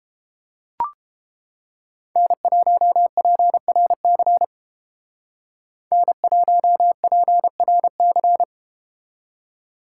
Morse code, heard as a keyed tone, about 33 wpm, 700 Hz. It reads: N1PRC N1PRC